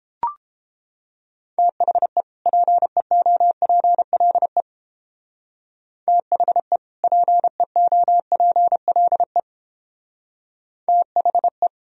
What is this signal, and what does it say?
Morse code, heard as a keyed tone, about 29 wpm, 700 Hz. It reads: THE PEOPLE THE PEOPLE THE